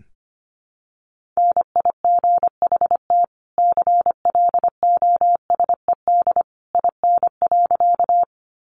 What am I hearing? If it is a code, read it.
Morse code, heard as a keyed tone, about 25 wpm, 700 Hz. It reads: NIGHT CLOSED IN.
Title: Sentences from Top 1400 Words 25wpm